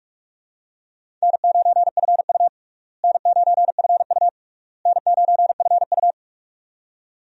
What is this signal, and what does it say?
Morse code, heard as a keyed tone, about 46 wpm, 700 Hz. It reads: N9FU N9FU N9FU
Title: USA Call Signs - 4 Characters - Repeated 3x - 45wpm